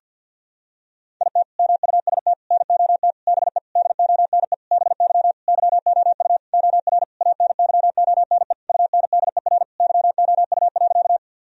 Morse code, heard as a keyed tone, about 52 wpm, 700 Hz. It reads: IT MUST NOT BE DONE BY YOU OR ANYONE UNDER YOU.
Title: Sentences from Top 1000 Words - Review Part 1 - 50wpm